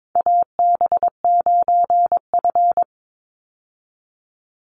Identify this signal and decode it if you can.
Morse code, heard as a keyed tone, about 22 wpm, 700 Hz. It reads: AB9F
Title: USA Call Signs 22wpm